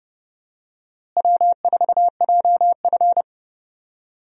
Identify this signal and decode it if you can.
Morse code, heard as a keyed tone, about 32 wpm, 700 Hz. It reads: W4JF
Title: USA Call Signs - 4 Characters - Repeated 3x - 30wpm